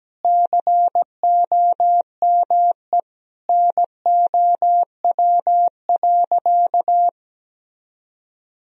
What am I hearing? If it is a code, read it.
Morse code, heard as a keyed tone, about 17 wpm, 700 Hz. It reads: COME NOW.
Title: Sentences from Top 200 Words and Abbreviations 17wpm